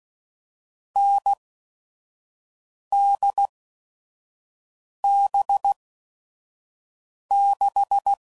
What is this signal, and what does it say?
Morse code, heard as a keyed tone, about 16 wpm, spaced as for 5 wpm, 800 Hz. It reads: NDB6
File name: Mixed / Short Runs / NDB NDB6